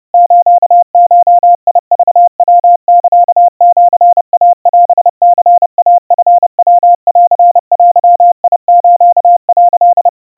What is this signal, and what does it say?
Morse code (Japanese Wabun code), heard as a keyed tone, about 30 wpm, 700 Hz. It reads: スゴクヤサシイオニイチヤンデス」
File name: kawa / Morse sample / kamyu 110 cpm 700 Hz Crispy